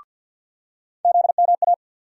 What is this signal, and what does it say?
Morse code, heard as a keyed tone, about 53 wpm, 700 Hz. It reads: ZMA
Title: USA Call Sign Postfixes - 3 Letters 50wpm